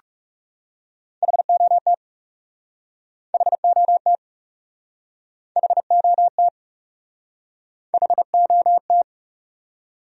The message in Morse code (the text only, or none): HOT HOT HOT HOT